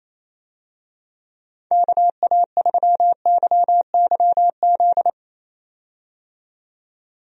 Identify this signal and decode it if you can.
Morse code, heard as a keyed tone, about 27 wpm, 700 Hz. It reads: KA3YYZ
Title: USA Call Signs - 6 Characters - 28wpm